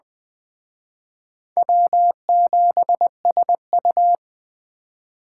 Morse code, heard as a keyed tone, about 20 wpm, 700 Hz. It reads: W7SU